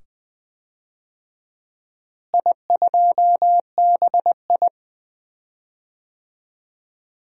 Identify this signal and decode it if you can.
Morse code, heard as a keyed tone, about 21 wpm, 700 Hz. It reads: I2BI
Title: DXCC-100 Call Signs - 4 Characters - Repeated 2x - 20wpm